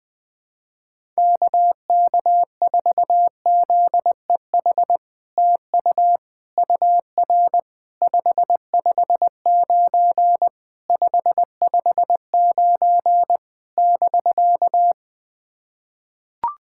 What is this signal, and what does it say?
Morse code, heard as a keyed tone, about 20 wpm, 700 Hz. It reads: KK4ZEH TU UR 559 559 <BK>
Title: SOTA QSO Practice 20wpm